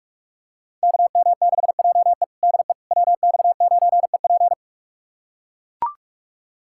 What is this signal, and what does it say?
Morse code, heard as a keyed tone, about 41 wpm, 700 Hz. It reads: KM6JE DE WX9EP
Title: USA Callsign DE Callsign 45wpm